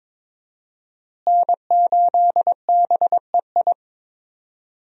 Morse code, heard as a keyed tone, about 22 wpm, 700 Hz. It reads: N8BEI
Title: USA Call Signs - 5 Characters - Repeated 3x - 22wpm